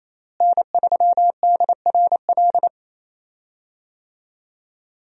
Morse code, heard as a keyed tone, about 28 wpm, 700 Hz. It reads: N3DRL
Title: USA Call Signs - 5 Characters - Repeated 2x - 28wpm